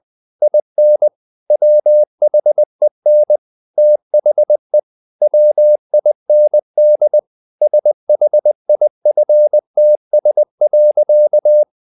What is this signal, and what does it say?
Morse code, heard as a keyed tone, about 20 wpm, 600 Hz. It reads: IN WHEN THE WIND SHIFTS.
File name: the old man and the sea / the old man and the sea part 1 20wpm